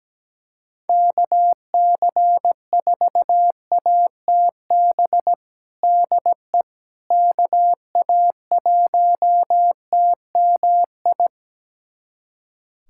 Morse code, heard as a keyed tone, about 17 wpm, 700 Hz. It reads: KC4ATB DE KA1TMI
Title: USA Callsign DE Callsign - 6 Characters - 17wpm